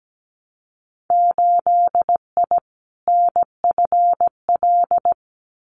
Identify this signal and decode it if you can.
Morse code, heard as a keyed tone, about 17 wpm, 700 Hz. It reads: 8I NFL
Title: Winter Field Day Exchanges Encore 17wpm